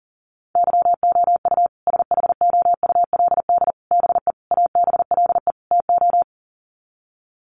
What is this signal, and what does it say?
Morse code, heard as a keyed tone, about 40 wpm, 700 Hz. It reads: YOU SHOULD BE ABLE TO
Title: Sets of 5 Words Encore 40wpm